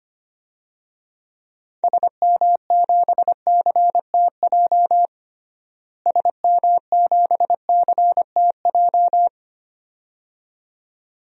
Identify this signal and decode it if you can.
Morse code, heard as a keyed tone, about 25 wpm, 700 Hz. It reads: SM7CTJ SM7CTJ
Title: DXCC-100 Call Signs - 6 Characters - Repeated 2x - 25wpm